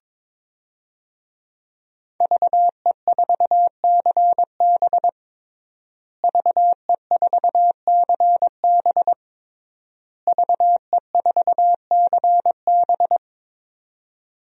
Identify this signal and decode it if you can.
Morse code, heard as a keyed tone, about 22 wpm, 700 Hz. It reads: VE4CB VE4CB VE4CB
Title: DXCC-100 Call Signs - 5 Characters - Repeated 3x - 22wpm